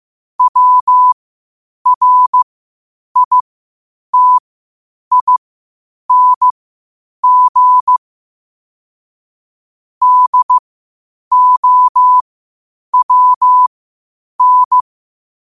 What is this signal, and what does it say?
Morse code, heard as a keyed tone, about 15 wpm, spaced as for 8 wpm, 1000 Hz. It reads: WRITING DOWN
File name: qotdfiles / 8wpm